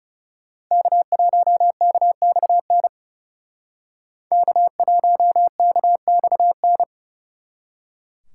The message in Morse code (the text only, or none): K1KXN K1KXN